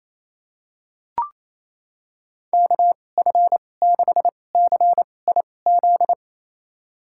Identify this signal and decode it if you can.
Morse code, heard as a keyed tone, about 28 wpm, spaced as for 20 wpm, 700 Hz. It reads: KF6CIZ
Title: USA Call Signs - 6 Characters - 2x Character Spacing - 28wpm